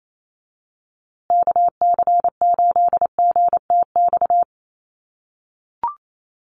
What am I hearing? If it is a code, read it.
Morse code, heard as a keyed tone, about 28 wpm, 700 Hz. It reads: KC8GTX
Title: USA Call Signs Repeated 2x 28wpm